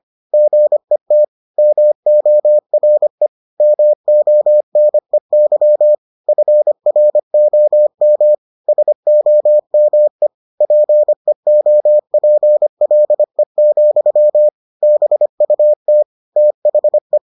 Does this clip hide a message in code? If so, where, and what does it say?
Yes: Morse code from 0.3 to 17.2 s, heard as a keyed tone, about 25 wpm, 600 Hz. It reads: GET MORE MONEY FROM SOME PEOPLE, BUT THE